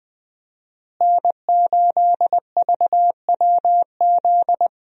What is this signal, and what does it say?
Morse code, heard as a keyed tone, about 20 wpm, 700 Hz. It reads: N8VWZ
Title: USA Call Signs - 5 Characters - Repeated 3x - 20wpm